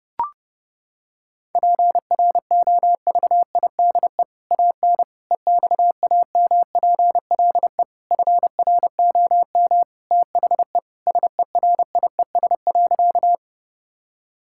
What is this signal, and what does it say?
Morse code, heard as a keyed tone, about 32 wpm, 700 Hz. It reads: PROVIDE AN EXAMPLE FROM THE SERIES.
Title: Sentences from Top 1000 Words - Review Part 5 - 30wpm